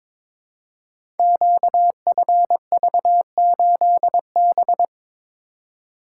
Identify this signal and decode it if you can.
Morse code, heard as a keyed tone, about 22 wpm, 700 Hz. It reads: QFV8B
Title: Code Groups - 5 Letter:Numbers - 22wpm